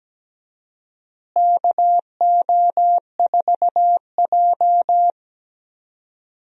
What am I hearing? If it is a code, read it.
Morse code, heard as a keyed tone, about 17 wpm, 700 Hz. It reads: KO4J
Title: USA Call Signs - 4 Characters - Repeated 2x - 17wpm